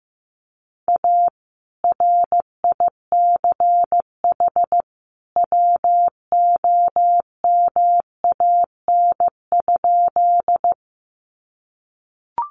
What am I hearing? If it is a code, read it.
Morse code, heard as a keyed tone, about 15 wpm, 700 Hz. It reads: A RICH WOMAN?
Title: Sentences from Top 1100 Words 15wpm